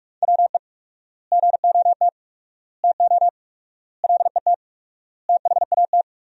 Morse code, heard as a keyed tone, about 45 wpm, 700 Hz. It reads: WE GOT TO LET THAT
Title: 4x Word-Spacing - Sentences from Top 300 Words - 45wpm